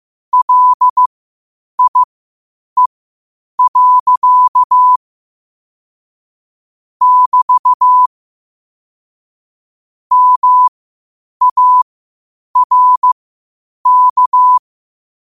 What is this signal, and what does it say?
Morse code, heard as a keyed tone, about 15 wpm, spaced as for 8 wpm, 1000 Hz. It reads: LIE. <BT> MARK
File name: qotdfiles / 8wpm